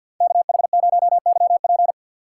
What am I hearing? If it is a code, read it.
Morse code, heard as a keyed tone, about 46 wpm, 700 Hz. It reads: KH0YP
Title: USA Call Signs - 5 Characters - 2x Character Spacing - 50wpm